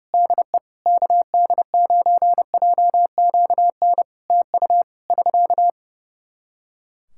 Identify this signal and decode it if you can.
Morse code, heard as a keyed tone, about 30 wpm, 700 Hz. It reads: DE KD9JQN TU <SK>
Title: Practice QSO - Exchange 4 - Ending - 30wpm